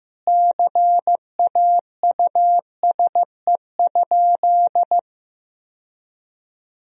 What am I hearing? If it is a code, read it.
Morse code, heard as a keyed tone, about 14 wpm, 700 Hz. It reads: CAUSE?
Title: Sentences from Top 1000 Words - Review Part 1 - 15wpm